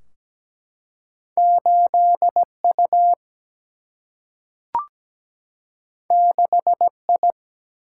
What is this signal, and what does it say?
Morse code, heard as a keyed tone, about 17 wpm, 700 Hz. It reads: 8 U 6 I
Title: Single-Letter Single-Number 17wpm